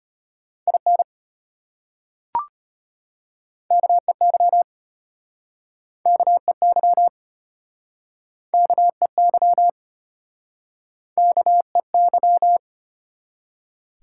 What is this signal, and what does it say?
Morse code, heard as a keyed tone, about 34 wpm, 700 Hz. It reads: IN KEY KEY KEY KEY